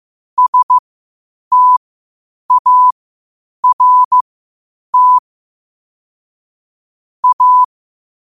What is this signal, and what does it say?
Morse code, heard as a keyed tone, about 15 wpm, spaced as for 8 wpm, 1000 Hz. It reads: START A
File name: qotdfiles / 8wpm